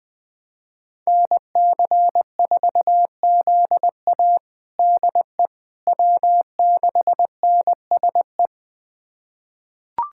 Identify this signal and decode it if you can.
Morse code, heard as a keyed tone, about 20 wpm, 700 Hz. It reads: NC4ZA DE W6NSE
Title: USA Callsign DE Callsign - 5 Characters - 20wpm